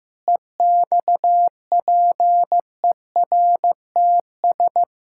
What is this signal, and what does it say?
Morse code, heard as a keyed tone, about 15 wpm, 700 Hz. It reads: EXPERTS
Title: Pangram Challenge v2 - 15wpm